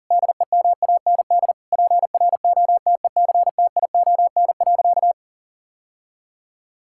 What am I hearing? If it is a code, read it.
Morse code, heard as a keyed tone, about 37 wpm, 700 Hz. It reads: DEMAND PROTECTION.
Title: Sentences from Top 1600 Words 40wpm